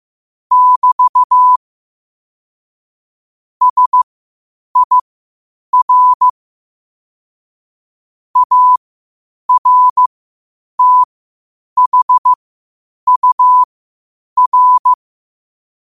Morse code, heard as a keyed tone, about 15 wpm, spaced as for 8 wpm, 1000 Hz. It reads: <BT> SIR ARTHUR
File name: qotdfiles / 8wpm